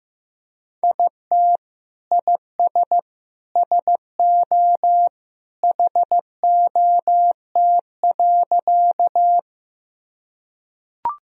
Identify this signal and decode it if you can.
Morse code, heard as a keyed tone, about 15 wpm, 700 Hz. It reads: IT IS SO HOT.
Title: Sentences from Top 1000 Words - Review Part 3 - 15wpm